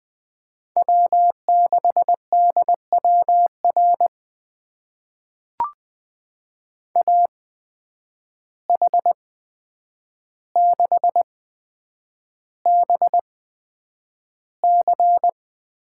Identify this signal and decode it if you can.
Morse code, heard as a keyed tone, about 20 wpm, spaced as for 9 wpm, 700 Hz. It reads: W6DWR AH6BC
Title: USA Call Signs - 8x Character Spacing - 20wpm